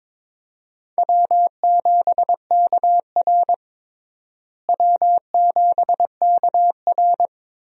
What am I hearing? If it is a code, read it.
Morse code, heard as a keyed tone, about 22 wpm, 700 Hz. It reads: W7KR W7KR
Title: USA Call Signs - 4 Characters - Repeated 2x - 22wpm